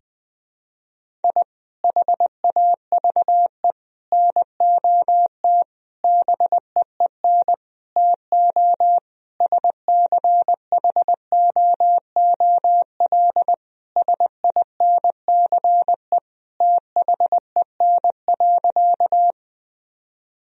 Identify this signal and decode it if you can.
Morse code, heard as a keyed tone, about 21 wpm, 700 Hz. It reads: I HAVE NOT BEEN TO SCHOOL SINCE THEN.
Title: Sentences from Top 1000 Words - Review Part 1 - 20wpm